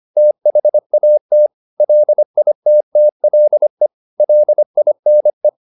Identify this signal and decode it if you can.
Morse code, heard as a keyed tone, about 25 wpm, 600 Hz. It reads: THAT LITTLE LINE
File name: the old man and the sea part 1 25wpm